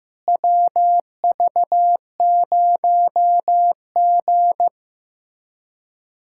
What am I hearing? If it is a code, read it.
Morse code, heard as a keyed tone, about 15 wpm, 700 Hz. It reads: WV0G
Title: USA Callsign DE Callsign - 4 Characters - 15wpm